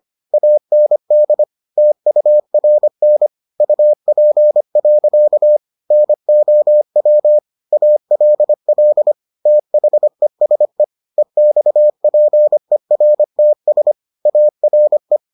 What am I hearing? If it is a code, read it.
Morse code, heard as a keyed tone, about 25 wpm, 600 Hz. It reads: AND TURN UP. NOW ALL THESE EXPERTS ARE